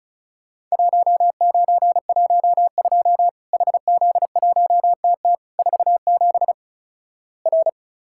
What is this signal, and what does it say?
Morse code, heard as a keyed tone, about 34 wpm, 700 Hz. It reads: 1912 HZ1TT 47 R
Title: Worked All Europe DX Contest - QTC Exchange - 35wpm